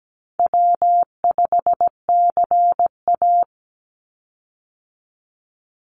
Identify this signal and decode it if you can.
Morse code, heard as a keyed tone, about 17 wpm, 700 Hz. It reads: W5CA
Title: USA Call Signs 17wpm